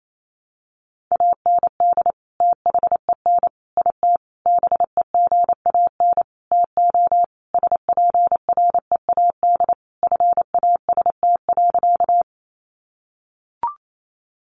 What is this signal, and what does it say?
Morse code, heard as a keyed tone, about 28 wpm, 700 Hz. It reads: AND THEN IT BEGAN TO SPREAD FAST.
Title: Sentences from Top 1200 Words 28wpm